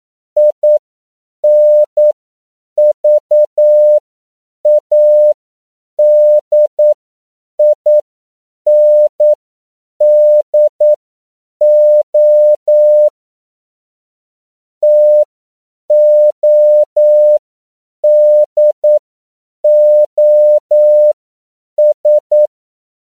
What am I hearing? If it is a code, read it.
Morse code, heard as a keyed tone, about 9 wpm, spaced as for 7 wpm, 600 Hz. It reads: INVADINDO TODOS